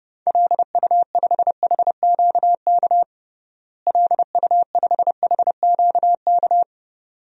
Morse code, heard as a keyed tone, about 29 wpm, 700 Hz. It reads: LU5HQK LU5HQK
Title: DXCC-100 Call Signs - 6 Characters - Repeated 2x - 30wpm